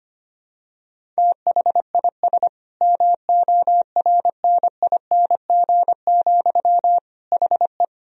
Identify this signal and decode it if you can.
Morse code, heard as a keyed tone, about 25 wpm, 700 Hz. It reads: THIS MORNING, HE
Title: Sentences from Top 1000 Words - Review Part 1 - 25wpm